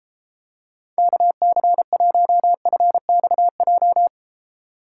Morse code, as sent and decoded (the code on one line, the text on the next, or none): -.- -.-. .---- ..-. -..- .---
KC1FXJ